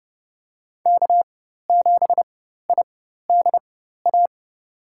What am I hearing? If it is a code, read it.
Morse code, heard as a keyed tone, about 30 wpm, spaced as for 14 wpm, 700 Hz. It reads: K7IDA
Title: USA Call Signs - 5 Characters - 4x Character Spacing - 30wpm